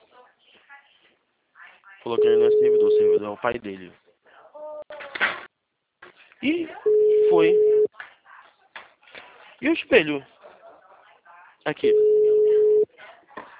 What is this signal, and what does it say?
Ringback tone (Europe, 425 Hz), with speech.